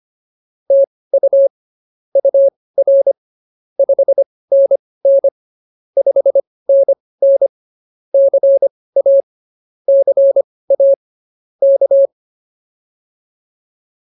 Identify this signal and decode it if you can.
Morse code, heard as a keyed tone, about 24 wpm, spaced as for 18 wpm, 550 Hz. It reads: TU UR 5NN 5NN CA CA K
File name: POTA109 18